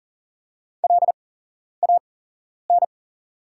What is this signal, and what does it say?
Morse code, heard as a keyed tone, about 40 wpm, spaced as for 11 wpm, 690 Hz. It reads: LAN
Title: 8x Character Spacing - Top 500 Words - 40wpm